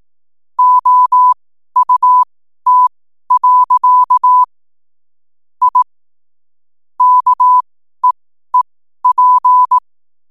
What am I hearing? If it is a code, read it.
Morse code, heard as a keyed tone, about 18 wpm, spaced as for 12 wpm, 1000 Hz. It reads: OUT. I KEEP
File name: n7net / 12wpm